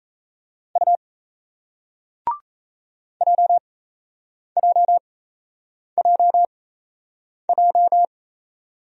Morse code, heard as a keyed tone, about 38 wpm, spaced as for 8 wpm, 700 Hz. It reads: U JJJJ